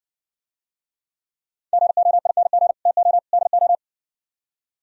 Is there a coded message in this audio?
Yes: Morse code from 1.7 to 3.7 s, heard as a keyed tone, about 62 wpm, 700 Hz. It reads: GOING TO DO